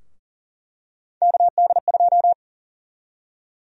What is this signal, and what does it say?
Morse code, heard as a keyed tone, about 40 wpm, 700 Hz. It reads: KD2
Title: USA Call Sign Prefixes - Repeated 2x - 40wpm